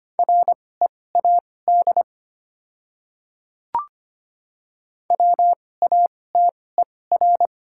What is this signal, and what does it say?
Morse code, heard as a keyed tone, about 25 wpm, spaced as for 18 wpm, 700 Hz. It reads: READ WATER
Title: 2x Character Spacing - Top 300 Words - 25wpm